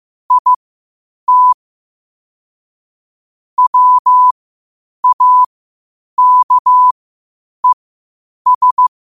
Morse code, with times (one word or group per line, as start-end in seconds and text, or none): 0.3-1.5 IT
3.6-8.9 WAKES